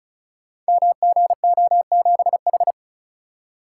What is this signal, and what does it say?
Morse code, heard as a keyed tone, about 36 wpm, 700 Hz. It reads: MGO7H
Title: Code Groups - 5 Letter:Numbers - 35wpm